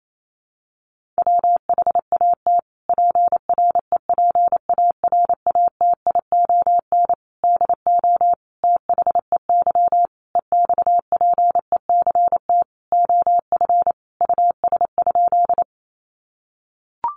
Morse code, heard as a keyed tone, about 28 wpm, 700 Hz. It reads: WHAT PREPARATION DO THEY EXPECT OF US?
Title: Sentences from Top 1400 Words 28wpm